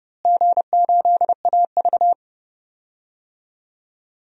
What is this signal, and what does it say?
Morse code, heard as a keyed tone, about 30 wpm, 700 Hz. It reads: G8AV
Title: DXCC-100 Call Signs - 4 Characters - Repeated 3x - 30wpm